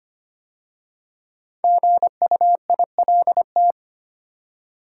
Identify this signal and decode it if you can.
Morse code, heard as a keyed tone, about 25 wpm, 700 Hz. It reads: GUILT
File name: 5 Letter Words - First 3 Letters Given - 25wpm